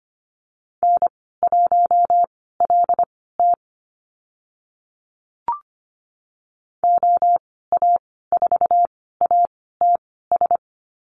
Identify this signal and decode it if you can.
Morse code, heard as a keyed tone, about 25 wpm, spaced as for 16 wpm, 690 Hz. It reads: N1LT OA4ATS
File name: Lesson 15g - Number 5 - Call Signs 25wpm